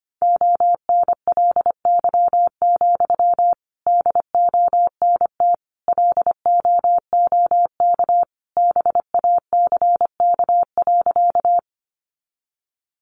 Morse code, heard as a keyed tone, about 25 wpm, 700 Hz. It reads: ONLY, DONT LOOK BACK.
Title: Sentences from Top 300 Words 25wpm